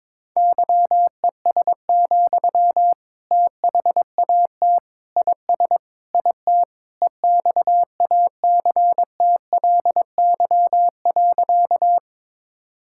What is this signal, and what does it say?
Morse code, heard as a keyed tone, about 22 wpm, 700 Hz. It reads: YES, THAT IS IT EXACTLY.
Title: Sentences from Top 1000 Words - Review Part 6 - 22wpm